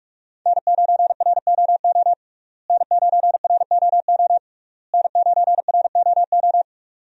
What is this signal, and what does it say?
Morse code, heard as a keyed tone, about 46 wpm, 700 Hz. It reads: N9ROO N9ROO N9ROO
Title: USA Call Signs - 5 Characters - Repeated 3x - 45wpm